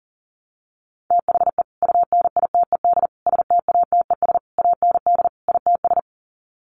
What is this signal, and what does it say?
Morse code, heard as a keyed tone, about 38 wpm, 700 Hz. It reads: THE UNITED STATES AND ITS
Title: Sets of 5 Words Encore 40wpm